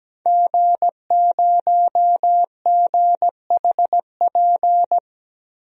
Morse code, heard as a keyed tone, about 17 wpm, 700 Hz. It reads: G0GHP
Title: DXCC-100 Call Signs - 5 Characters - Repeated 2x - 17wpm